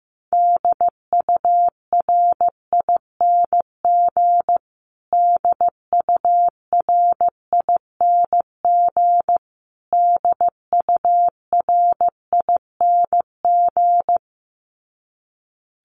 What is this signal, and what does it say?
Morse code, heard as a keyed tone, about 15 wpm, 700 Hz. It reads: DURING DURING DURING
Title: Top 500 Words Repeated 15wpm